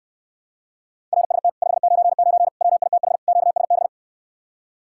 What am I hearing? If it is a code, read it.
Morse code, heard as a keyed tone, about 69 wpm, 700 Hz. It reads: RST 599 WITH QSB